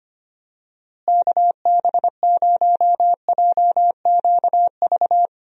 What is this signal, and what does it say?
Morse code, heard as a keyed tone, about 25 wpm, 700 Hz. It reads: KB0JQV
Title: USA Call Signs - 6 Characters - Repeated 3x - 25wpm